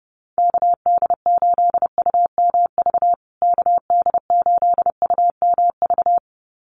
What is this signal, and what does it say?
Morse code, heard as a keyed tone, about 30 wpm, 700 Hz. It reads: KD8UMV KD8UMV
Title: USA Call Signs Repeated 2x 30wpm